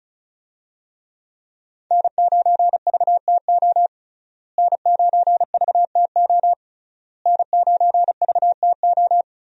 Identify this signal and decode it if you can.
Morse code, heard as a keyed tone, about 36 wpm, 700 Hz. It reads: N9VTO N9VTO N9VTO
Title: USA Call Signs - 5 Characters - Repeated 3x - 35wpm